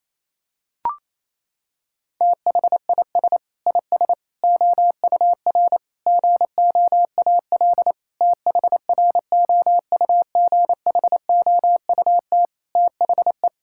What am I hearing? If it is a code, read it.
Morse code, heard as a keyed tone, about 28 wpm, 700 Hz. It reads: THIS IS OUR GOAL THROUGHOUT THE
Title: Sentences from Top 1000 Words - Review Part 5 - 28wpm